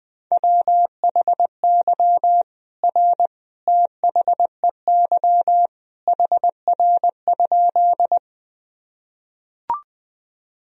Morse code, heard as a keyed tone, about 20 wpm, 700 Hz. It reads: WHY R THEY HR?
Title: Sentences from Top 200 Words and Abbreviations 20wpm